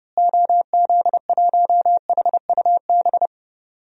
Morse code, heard as a keyed tone, about 32 wpm, 700 Hz. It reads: OZ1HUB
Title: DXCC-100 Call Signs - 6 Characters - Repeated 2x - 30wpm